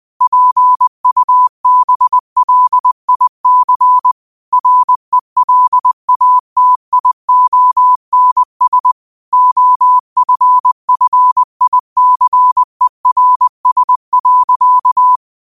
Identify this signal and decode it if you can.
Morse code, heard as a keyed tone, about 20 wpm, 1000 Hz. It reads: PUBLIC RELATIONS OFFICERS.
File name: qotdfiles / 20wpm